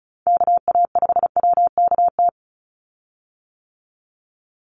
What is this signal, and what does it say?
Morse code, heard as a keyed tone, about 34 wpm, 700 Hz. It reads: KA5WKT
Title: USA Call Signs 35wpm